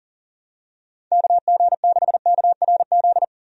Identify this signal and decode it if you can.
Morse code, heard as a keyed tone, about 37 wpm, 700 Hz. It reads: KG6KRZ
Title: USA Call Signs - 6 Characters - 40wpm